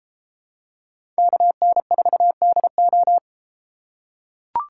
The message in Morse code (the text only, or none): KN4DO